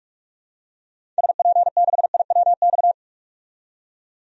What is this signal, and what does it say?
Morse code, heard as a keyed tone, about 47 wpm, 700 Hz. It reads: SP6IWX
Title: DXCC-100 Call Signs - 6 Characters - 45wpm